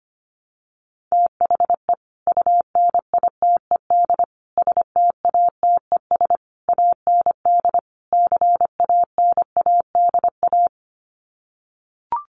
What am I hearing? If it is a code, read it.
Morse code, heard as a keyed tone, about 25 wpm, 700 Hz. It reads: THE UNITED STATES AND CANADA
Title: Sets of 5 Words 25wpm